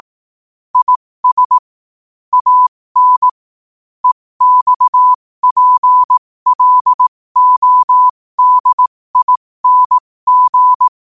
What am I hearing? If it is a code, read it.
Morse code, heard as a keyed tone, about 18 wpm, spaced as for 15 wpm, 1000 Hz. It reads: IS AN EXPLODING